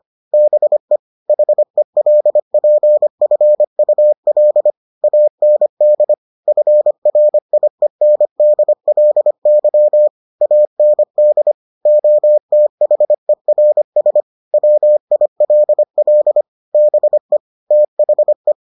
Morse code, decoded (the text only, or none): BE HELPFUL AND FRIENDLY AND OTHERS WILL BE THE